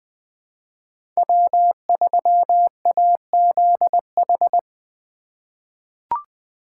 Morse code, heard as a keyed tone, about 20 wpm, 700 Hz. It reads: W3AZH
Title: USA Call Signs - 5 Characters - 3x Character Spacing - 20wpm